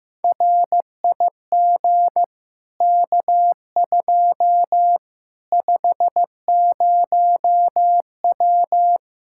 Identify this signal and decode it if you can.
Morse code, heard as a keyed tone, about 15 wpm, 700 Hz. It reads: RIG K2 50W
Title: Practice QSO - Exchange 2 - Info - 15wpm